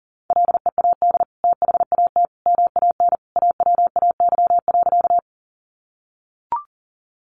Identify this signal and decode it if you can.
Morse code, heard as a keyed tone, about 40 wpm, 700 Hz. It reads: LEAD THAT MAN AWAY.
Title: Sentences from Top 600 Words 40wpm